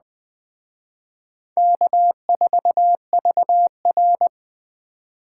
Morse code, heard as a keyed tone, about 22 wpm, 700 Hz. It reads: K4VR